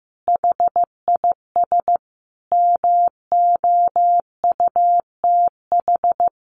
Morse code, heard as a keyed tone, about 15 wpm, 700 Hz. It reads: HIS MOUTH
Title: A Sound Of Thunder 15wpm